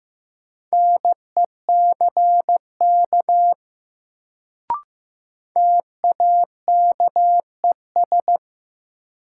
Morse code, heard as a keyed tone, about 15 wpm, 700 Hz. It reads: NECK TAKES
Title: Sets of 5 Words Encore - New Word Drill - 15wpm